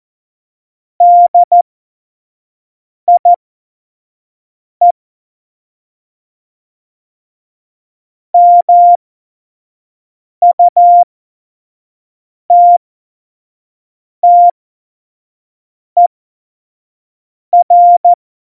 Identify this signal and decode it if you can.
Morse code, heard as a keyed tone, about 14 wpm, spaced as for 5 wpm, 700 Hz. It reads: DIE MUTTER
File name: Morsekurs / Witze / Witz 29. August 17, Tempo 25 F70